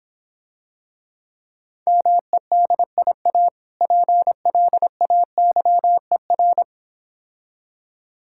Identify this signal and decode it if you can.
Morse code, heard as a keyed tone, about 26 wpm, 690 Hz. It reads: MEDIA PLAYER